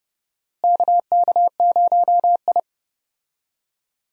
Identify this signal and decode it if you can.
Morse code, heard as a keyed tone, about 30 wpm, 700 Hz. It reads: KK0I
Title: USA Call Signs - 4 Characters - Repeated 3x - 30wpm